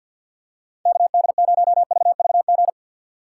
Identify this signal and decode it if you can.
Morse code, heard as a keyed tone, about 53 wpm, 690 Hz. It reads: KD0VVG
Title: USA Call Signs - 4x Character Spacing - 50wpm